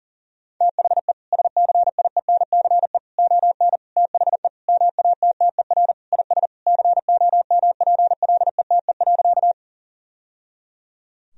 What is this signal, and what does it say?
Morse code, heard as a keyed tone, about 40 wpm, 700 Hz. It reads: THE SCIENCE ON THE MATTER IS COMPLETE.
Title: Sentences from Top 1000 Words - Review Part 2 - 40wpm